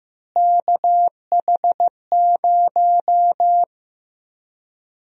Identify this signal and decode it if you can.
Morse code, heard as a keyed tone, about 15 wpm, 700 Hz. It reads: KH0
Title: USA Call Sign Prefixes - Repeated 2x - 15wpm